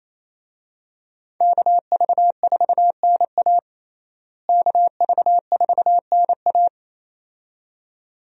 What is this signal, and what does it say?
Morse code, heard as a keyed tone, about 28 wpm, 700 Hz. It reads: KV4NA KV4NA
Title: USA Call Signs - 5 Characters - Repeated 2x - 28wpm